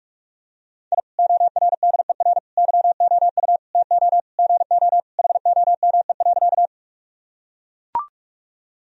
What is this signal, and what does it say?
Morse code, heard as a keyed tone, about 41 wpm, 700 Hz. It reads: I ORDER YOU TO GO HOME.
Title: Sentences from Top 1000 Words - Review Part 2 - 45wpm